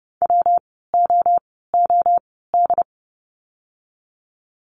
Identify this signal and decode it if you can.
Morse code, heard as a keyed tone, about 31 wpm, spaced as for 17 wpm, 700 Hz. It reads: WOOD
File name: course / 30:10wpm / 115 Lesson 19f - Letter w - Words 30wpm